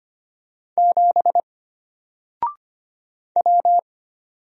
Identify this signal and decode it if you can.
Morse code, heard as a keyed tone, about 25 wpm, 700 Hz. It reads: 7 W
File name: Single-Letter-Number-25wpm